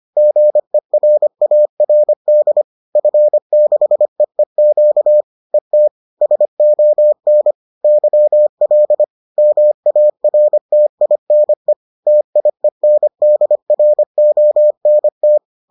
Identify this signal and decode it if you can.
Morse code, heard as a keyed tone, about 25 wpm, 600 Hz. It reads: GERARD F6EEQ ET SON YL MARTINE TIENDRONT
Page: bulletin UFT sous forme texte et en CW